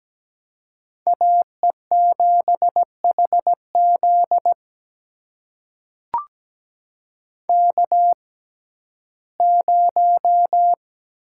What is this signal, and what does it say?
Morse code, heard as a keyed tone, about 17 wpm, spaced as for 12 wpm, 700 Hz. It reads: AE7HZ K0
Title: USA Call Signs - 6x Character Spacing - 17wpm